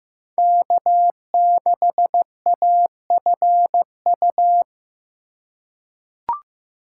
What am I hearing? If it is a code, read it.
Morse code, heard as a keyed tone, about 15 wpm, 700 Hz. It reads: K6AFU
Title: USA Call Signs - 5 Characters - 3x Character Spacing - 15wpm